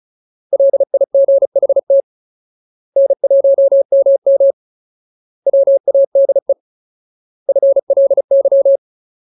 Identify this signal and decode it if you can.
Morse code, heard as a keyed tone, about 34 wpm, 550 Hz. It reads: LIGHT N1MM WADE FLY
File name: Ultimate Word List / 35WPM Ultimate